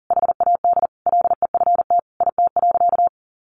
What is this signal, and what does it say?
Morse code, heard as a keyed tone, about 40 wpm, 700 Hz. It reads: HAD LEFT IT.
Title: A Sound Of Thunder 40wpm